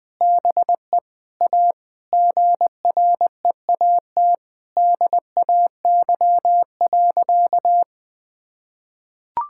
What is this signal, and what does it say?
Morse code, heard as a keyed tone, about 20 wpm, 700 Hz. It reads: BE A GREAT DAY.
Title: Sentences with 4-Word Phrases - 20wpm